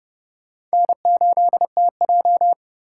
Morse code, heard as a keyed tone, about 36 wpm, 700 Hz. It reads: N8TJ
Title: Speed Racing - USA 1x2 Call Signs 30wpm